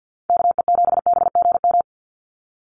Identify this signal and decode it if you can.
Morse code, heard as a keyed tone, about 50 wpm, 700 Hz. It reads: KE7BGM
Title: NA SOTA Activator Callsigns 50wpm